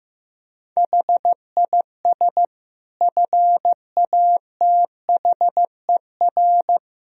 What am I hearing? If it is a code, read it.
Morse code, heard as a keyed tone, about 16 wpm, 700 Hz. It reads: HIS FATHER
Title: Sentences from Top 1000 Words - Review Part 4 - 15wpm